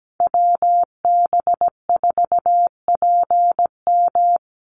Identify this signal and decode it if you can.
Morse code, heard as a keyed tone, about 17 wpm, 700 Hz. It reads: WB4PM
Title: USA Call Signs Repeated 2x 17wpm